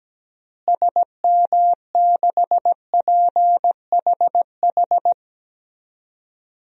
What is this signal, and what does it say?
Morse code, heard as a keyed tone, about 17 wpm, 700 Hz. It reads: SM6PHH
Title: DXCC-100 Call Signs - 6 Characters - Repeated 2x - 17wpm